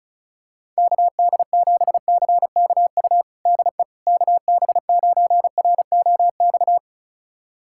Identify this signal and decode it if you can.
Morse code, heard as a keyed tone, about 36 wpm, 700 Hz. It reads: KD7CKU DE KB9ROX
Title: USA Callsign DE Callsign 35wpm